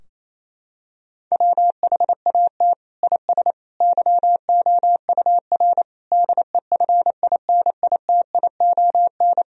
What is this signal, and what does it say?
Morse code, heard as a keyed tone, about 28 wpm, 700 Hz. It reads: WHAT IS YOUR DEFINITION
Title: Sentences from Top 1000 Words - Review Part 4 - 28wpm